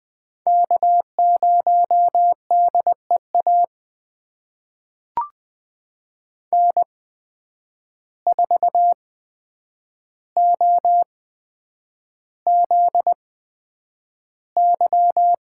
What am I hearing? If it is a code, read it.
Morse code, heard as a keyed tone, about 20 wpm, spaced as for 9 wpm, 700 Hz. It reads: K0DEA N4OZY
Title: USA Call Signs - 5 Characters - 8x Character Spacing - 20wpm